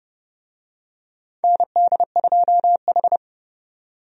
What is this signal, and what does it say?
Morse code, heard as a keyed tone, about 31 wpm, 700 Hz. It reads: ND2H
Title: USA Call Signs - 4 Characters - 30wpm